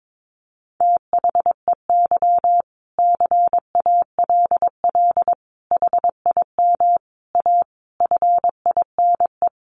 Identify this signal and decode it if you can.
Morse code, heard as a keyed tone, about 22 wpm, 700 Hz. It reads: THEY CALL HIM A FINE
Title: Sentences from Top 500 Words 22wpm